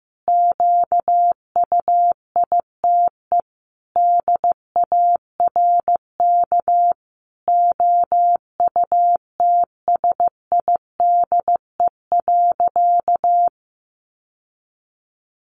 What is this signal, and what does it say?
Morse code, heard as a keyed tone, about 15 wpm, 700 Hz. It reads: QUITE DARK OUTSIDE.
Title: Sentences from Top 900 Words 15wpm